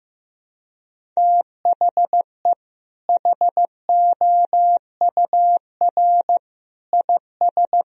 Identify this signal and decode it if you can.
Morse code, heard as a keyed tone, about 15 wpm, 700 Hz. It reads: THE HOUR IS
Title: Sentences from Top 1000 Words - Review Part 1 - 15wpm